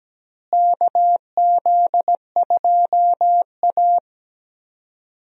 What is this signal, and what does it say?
Morse code, heard as a keyed tone, about 17 wpm, 700 Hz. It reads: KZ2A
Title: USA Call Signs - 4 Characters - Repeated 3x - 17wpm